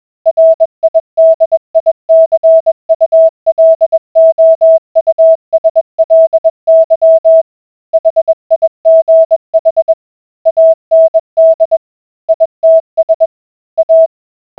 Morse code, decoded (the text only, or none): RIDICULOUSLY HIGH AND ITS A